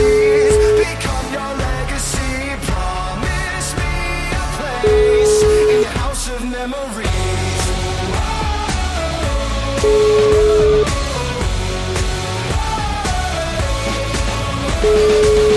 Ringback tone (Europe, 425 Hz), with noise.